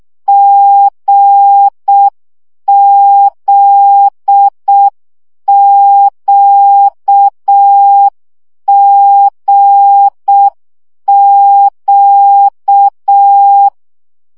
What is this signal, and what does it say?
Morse code, heard as a keyed tone, about 6 wpm, 800 Hz. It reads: GZQGQ
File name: CW gyakorlas g z q 5 baud